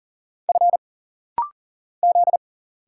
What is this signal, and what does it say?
Morse code, heard as a keyed tone, about 40 wpm, 700 Hz. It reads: F Z
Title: Single Letters - Rapid-Fire 40wpm